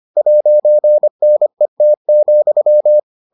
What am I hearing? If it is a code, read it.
Morse code, heard as a keyed tone, about 25 wpm, 600 Hz. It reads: 'NET,